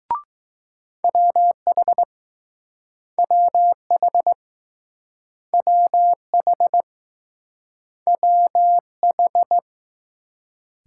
Morse code, heard as a keyed tone, about 20 wpm, 700 Hz. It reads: WH WH WH WH